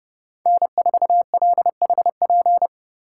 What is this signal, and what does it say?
Morse code, heard as a keyed tone, about 29 wpm, 700 Hz. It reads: N4LHP